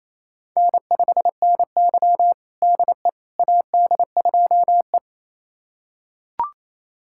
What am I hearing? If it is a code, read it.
Morse code, heard as a keyed tone, about 28 wpm, 700 Hz. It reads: N5NY DE AD2E
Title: USA Callsign DE Callsign - 4 Characters - 28wpm